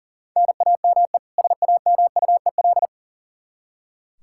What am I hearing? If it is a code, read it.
Morse code, heard as a keyed tone, about 40 wpm, 700 Hz. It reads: NAME SAMUEL
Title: Practice QSO - Exchange 1 - Beginning - 40wpm